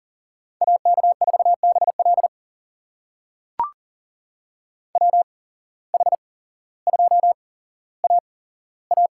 Morse code, heard as a keyed tone, about 40 wpm, spaced as for 17 wpm, 700 Hz. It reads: AK4BL WH2AA
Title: USA Call Signs - 5 Characters - 8x Character Spacing - 40wpm